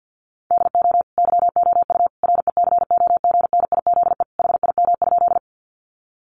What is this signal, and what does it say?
Morse code, heard as a keyed tone, about 50 wpm, 700 Hz. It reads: DO YOU RECOGNIZE HIM?
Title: Sentences from Top 1400 Words 50wpm